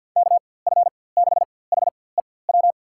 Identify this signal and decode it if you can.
Morse code, heard as a keyed tone, about 52 wpm, spaced as for 23 wpm, 700 Hz. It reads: KF6HEW
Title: USA Call Signs - 6 Characters - 4x Character Spacing - 50wpm